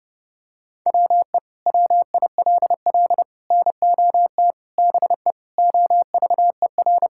Morse code, heard as a keyed tone, about 30 wpm, 700 Hz. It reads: WE WILL NOT BE OVER
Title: Sentences from Top 1000 Words - Review Part 5 - 30wpm